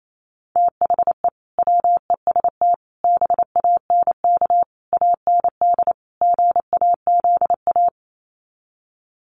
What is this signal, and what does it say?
Morse code, heard as a keyed tone, about 28 wpm, 700 Hz. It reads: THE WEST BANK AND GAZA
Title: Sets of 5 Words 28wpm